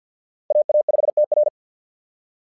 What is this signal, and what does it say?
Morse code, heard as a keyed tone, about 48 wpm, 600 Hz. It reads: AA5TR
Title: SOTA QSO Practice 50wpm